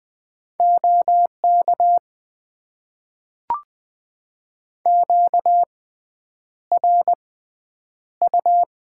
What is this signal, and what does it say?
Morse code, heard as a keyed tone, about 20 wpm, spaced as for 9 wpm, 700 Hz. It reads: OK QRU
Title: Top 90 QSO Elements - 6x Character Spacing - 20wpm